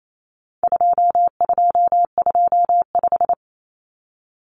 Morse code, heard as a keyed tone, about 28 wpm, 700 Hz. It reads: 2225
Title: Common Cut Numbers - 4-Digit 28wpm